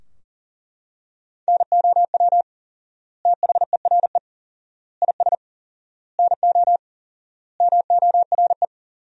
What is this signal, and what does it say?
Morse code, heard as a keyed tone, about 40 wpm, 700 Hz. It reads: NOW THERE IS NO MORE
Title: 4x Word-Spacing - Sentences from Top 100 Words - 40wpm